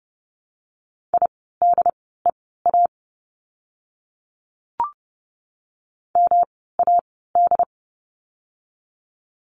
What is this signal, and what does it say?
Morse code, heard as a keyed tone, about 30 wpm, spaced as for 17 wpm, 700 Hz. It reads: IDEA MAD
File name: course / 30:10wpm / 108 Lesson 18f - Letter m - Words 30wpm